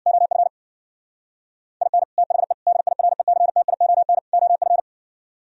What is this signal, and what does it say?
Morse code, heard as a keyed tone, about 61 wpm, 700 Hz. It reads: OF IN THE DIRECTION OF
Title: Speed-Racing - Sets of 4 Words 50wpm